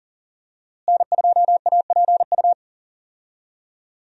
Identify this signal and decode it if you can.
Morse code, heard as a keyed tone, about 40 wpm, 700 Hz. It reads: N2APU
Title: USA Call Signs - 5 Characters - Repeated 2x - 40wpm